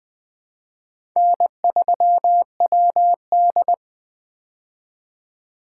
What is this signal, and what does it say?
Morse code, heard as a keyed tone, about 20 wpm, 700 Hz. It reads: N3WD
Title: 2023 POTA Activator Call Signs - 20wpm